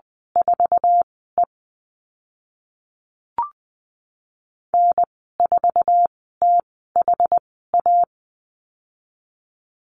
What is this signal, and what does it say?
Morse code, heard as a keyed tone, about 20 wpm, spaced as for 14 wpm, 700 Hz. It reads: 4E N4THA
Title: 076 Lesson 13g - Letter L - Call Signs 20wpm